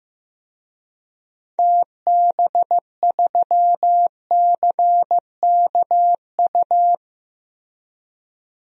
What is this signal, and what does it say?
Morse code, heard as a keyed tone, about 15 wpm, 700 Hz. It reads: TB3CKU
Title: DXCC-100 Call Signs - 6 Characters - Repeated 2x - 15wpm